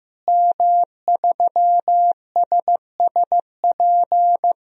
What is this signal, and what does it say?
Morse code, heard as a keyed tone, about 15 wpm, 700 Hz. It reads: M3SSP
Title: DXCC-100 Call Signs - 5 Characters - Repeated 3x - 15wpm